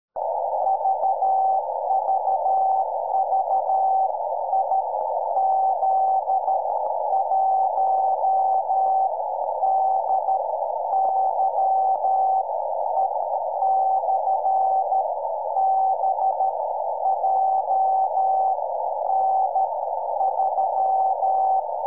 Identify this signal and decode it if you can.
Morse code, heard as a keyed tone, about 13 wpm, spaced as for 10 wpm, 770 Hz, with noise. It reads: VVV IZ1DYE/B JN3
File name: iw0ffk / 2008 08 05 IZ1DYE BEACON